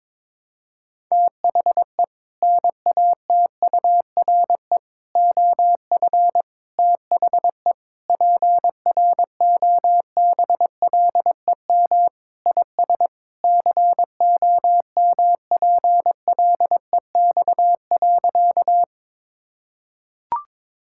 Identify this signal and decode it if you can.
Morse code, heard as a keyed tone, about 22 wpm, 700 Hz. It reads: THE NATURE OF THE PROBLEM IS COMPLEX.
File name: practice / 22wpm / Sentences with 4-Word Phrases - 22wpm